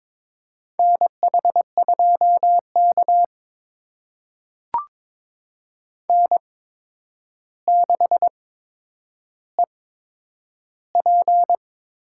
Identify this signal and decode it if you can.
Morse code, heard as a keyed tone, about 21 wpm, spaced as for 9 wpm, 700 Hz. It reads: NH2K N6EP
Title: USA Call Signs - 4 Characters - 8x Character Spacing - 22wpm